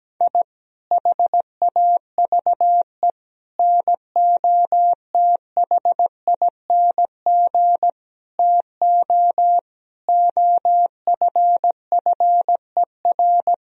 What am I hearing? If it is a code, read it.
Morse code, heard as a keyed tone, about 17 wpm, 700 Hz. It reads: I HAVE NOTHING TO OFFER
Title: Sentences from Top 1000 Words - Review Part 4 - 17wpm